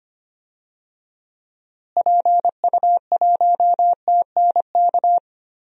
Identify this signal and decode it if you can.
Morse code, heard as a keyed tone, about 25 wpm, 700 Hz. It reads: PU1TNK
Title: DXCC-100 Call Signs - 6 Characters - Repeated 3x - 25wpm